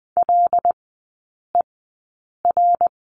Morse code, heard as a keyed tone, about 20 wpm, spaced as for 8 wpm, 700 Hz. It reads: LER
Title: 114 Lesson 19e - Letter w - Three Characters 20wpm